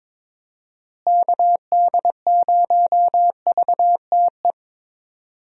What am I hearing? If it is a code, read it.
Morse code, heard as a keyed tone, about 22 wpm, 700 Hz. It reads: KD0VTE
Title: USA Call Signs - 6 Characters - Repeated 3x - 22wpm